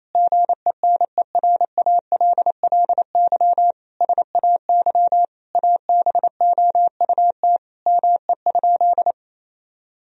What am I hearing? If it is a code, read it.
Morse code, heard as a keyed tone, about 28 wpm, 700 Hz. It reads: GENERALLY SAY ABOUT ME?
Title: Sentences from Top 1000 Words - Review Part 4 - 28wpm